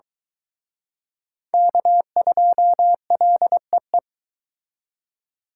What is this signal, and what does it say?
Morse code, heard as a keyed tone, about 22 wpm, 700 Hz. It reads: K2LEE